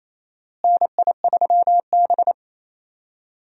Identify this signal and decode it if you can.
Morse code, heard as a keyed tone, about 28 wpm, 700 Hz. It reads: NI3B
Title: USA Call Signs - 4 Characters - Repeated 2x - 28wpm